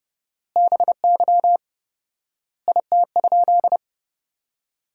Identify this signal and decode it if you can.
Morse code, heard as a keyed tone, about 32 wpm, 700 Hz. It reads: BY IT?
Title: 4x Word-Spacing - Sentences from Top 200 Words - 30wpm